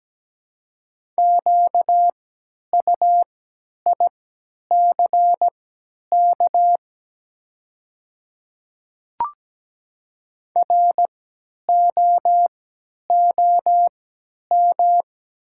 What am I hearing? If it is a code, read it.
Morse code, heard as a keyed tone, about 17 wpm, spaced as for 10 wpm, 700 Hz. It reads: QUICK ROOM
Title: 3x Character Spacing - Top 500 Words - 17wpm